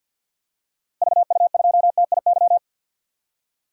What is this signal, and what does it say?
Morse code, heard as a keyed tone, about 49 wpm, 700 Hz. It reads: VU2TIY
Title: DXCC-100 Call Signs - 6 Characters - Repeated 3x - 50wpm